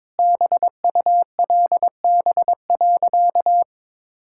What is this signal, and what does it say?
Morse code, heard as a keyed tone, about 22 wpm, 690 Hz. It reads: BULB.
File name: practice / 22wpm / Easy Sentences with Hints - 22wpm